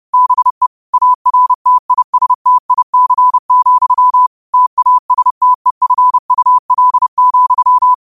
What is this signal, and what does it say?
Morse code, heard as a keyed tone, about 30 wpm, 1000 Hz. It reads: BE ARTISTIC, TASTEFUL,